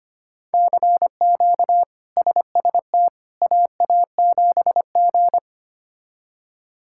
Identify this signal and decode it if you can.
Morse code, heard as a keyed tone, about 25 wpm, 700 Hz. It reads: CQ SST AA7G
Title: K1USN SST QSO Practice - 25wpm